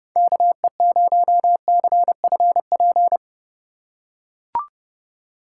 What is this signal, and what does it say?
Morse code, heard as a keyed tone, about 29 wpm, 700 Hz. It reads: KE0CFP
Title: USA Call Signs - 6 Characters - 4x Character Spacing - 30wpm